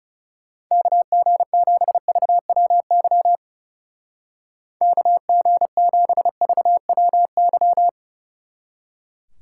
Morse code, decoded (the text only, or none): KG7VWY KG7VWY